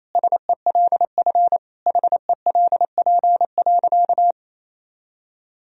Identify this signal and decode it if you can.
Morse code, heard as a keyed tone, about 28 wpm, 700 Hz. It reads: SELF HELP.
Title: Sentences from Top 1000 Words - Review Part 1 - 28wpm